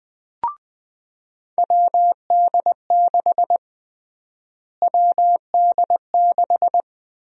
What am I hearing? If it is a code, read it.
Morse code, heard as a keyed tone, about 20 wpm, 700 Hz. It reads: WD6 WD6
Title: USA Call Sign Prefixes - Repeated 2x - 20wpm